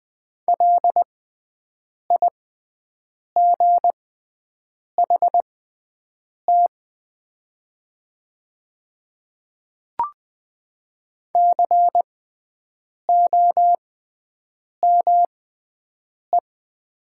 Morse code, heard as a keyed tone, about 20 wpm, spaced as for 7 wpm, 700 Hz. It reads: LIGHT COME
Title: 6x Character Spacing - Top 200 Words - 20wpm